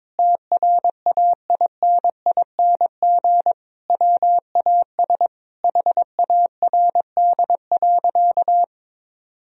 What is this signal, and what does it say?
Morse code, heard as a keyed tone, about 22 wpm, 700 Hz. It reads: TRAINING WAS HARD.
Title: Sentences from Top 1000 Words - Review Part 4 - 22wpm